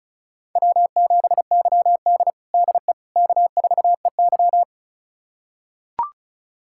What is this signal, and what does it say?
Morse code, heard as a keyed tone, about 36 wpm, 700 Hz. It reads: W7YD DE K4EY
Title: USA Callsign DE Callsign - 4 Characters - 35wpm